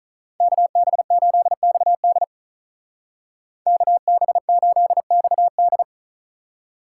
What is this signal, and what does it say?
Morse code, heard as a keyed tone, about 43 wpm, 700 Hz. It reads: KB8XD KB8XD
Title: Speed Racing - USA Call Signs 35wpm